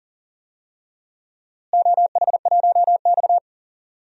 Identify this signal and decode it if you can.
Morse code, heard as a keyed tone, about 38 wpm, 700 Hz. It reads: OH1X
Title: DXCC-100 Call Signs - 4 Characters - Repeated 3x - 40wpm